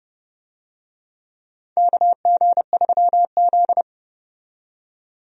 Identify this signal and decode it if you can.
Morse code, heard as a keyed tone, about 31 wpm, 700 Hz. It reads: KG3Z
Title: USA Call Signs - 4 Characters - 30wpm